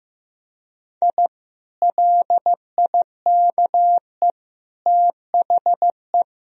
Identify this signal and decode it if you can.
Morse code, heard as a keyed tone, about 15 wpm, 700 Hz. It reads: I LIKE THE
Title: Sentences from Top 1000 Words - Review Part 2 - 15wpm